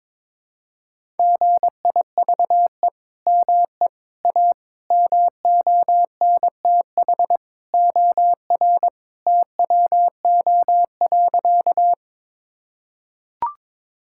Morse code, heard as a keyed tone, about 22 wpm, 700 Hz. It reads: GIVE ME A MONTH OR TWO.
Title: Sentences from Top 1000 Words - Review Part 5 - 22wpm